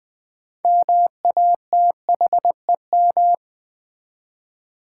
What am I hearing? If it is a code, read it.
Morse code, heard as a keyed tone, about 20 wpm, 700 Hz. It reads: MATHEM
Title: Build A Word - 6 Letters At A Time - 12-Letter Words - 20wpm